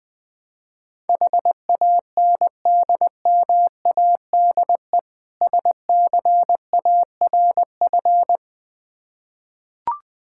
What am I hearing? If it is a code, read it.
Morse code, heard as a keyed tone, about 21 wpm, 700 Hz. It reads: HANDMADE SCARF
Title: Difficult Words with Hints - 20wpm